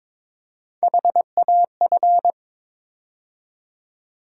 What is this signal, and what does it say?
Morse code, heard as a keyed tone, about 22 wpm, 700 Hz. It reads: HAF
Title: USA Call Sign Postfixes - 3 Letters 22wpm